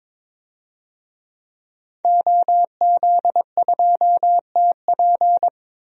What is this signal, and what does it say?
Morse code, heard as a keyed tone, about 22 wpm, 700 Hz. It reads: OZ2TP
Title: DXCC-100 Call Signs - 5 Characters - 22wpm